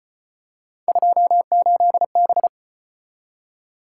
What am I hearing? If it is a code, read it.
Morse code, heard as a keyed tone, about 34 wpm, 700 Hz. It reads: 28B